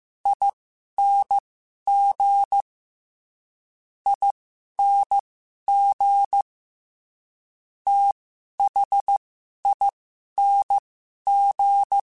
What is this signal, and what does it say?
Morse code, heard as a keyed tone, about 15 wpm, spaced as for 11 wpm, 800 Hz. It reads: ING ING THING
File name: Special INGs-mp3s / Special INGs-mp3s / SlowToFaster ing @10